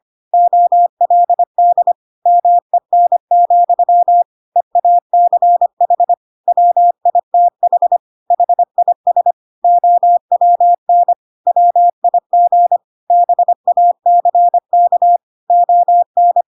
Morse code, heard as a keyed tone, about 25 wpm, 700 Hz. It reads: OLD MEN, EACH WITH HIS OWN WIG BACK ON
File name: Pinocchio / Pinocchio ch2 25 25 ik1hge